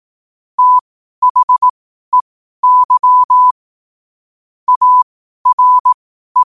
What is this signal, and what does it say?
Morse code, heard as a keyed tone, about 18 wpm, spaced as for 12 wpm, 1000 Hz. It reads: THEY ARE
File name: qotdfiles / 12wpm